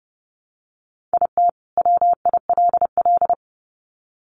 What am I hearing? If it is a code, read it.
Morse code, heard as a keyed tone, about 30 wpm, 700 Hz. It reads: IT WILL
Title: Sets of 2 Words 30wpm